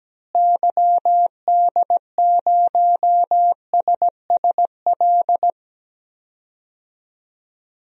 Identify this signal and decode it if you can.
Morse code, heard as a keyed tone, about 17 wpm, 700 Hz. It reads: YD0SSL
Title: DXCC-100 Call Signs - 6 Characters - Repeated 2x - 17wpm